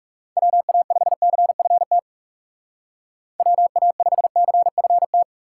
Morse code, heard as a keyed tone, about 35 wpm, 700 Hz. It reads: WA5CFT WA5CFT
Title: Speed Racing - USA 2x3 Call Signs 30wpm